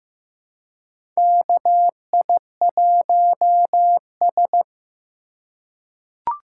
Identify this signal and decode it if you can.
Morse code, heard as a keyed tone, about 15 wpm, 700 Hz. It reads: KI1S
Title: USA Call Signs - 4 Characters - 3x Character Spacing - 15wpm